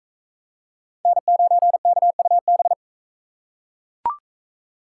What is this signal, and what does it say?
Morse code, heard as a keyed tone, about 44 wpm, 700 Hz. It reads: N9KUB